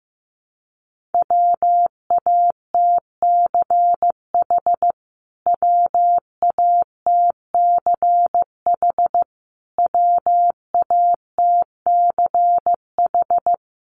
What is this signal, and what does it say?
Morse code, heard as a keyed tone, about 15 wpm, 700 Hz. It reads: WATCH WATCH WATCH
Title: Top 500 Words Repeated 15wpm